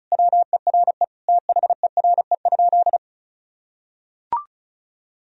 Morse code, heard as a keyed tone, about 36 wpm, 700 Hz. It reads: WERE THERE?
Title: Sentences from Top 1000 Words - Review Part 1 - 35wpm